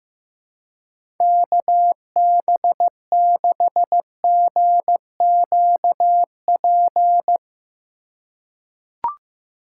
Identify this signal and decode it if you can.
Morse code, heard as a keyed tone, about 15 wpm, 700 Hz. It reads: KB6GQP
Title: USA Call Signs - 6 Characters - 2x Character Spacing - 15wpm